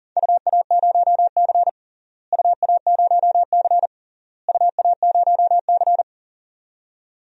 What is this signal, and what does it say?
Morse code, heard as a keyed tone, about 40 wpm, 700 Hz. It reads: UA0C UA0C UA0C
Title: DXCC-100 Call Signs - 4 Characters - Repeated 3x - 40wpm